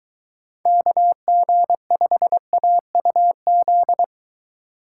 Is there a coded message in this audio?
Yes: Morse code from 0.7 to 4.0 s, heard as a keyed tone, about 21 wpm, 700 Hz. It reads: KG5AUZ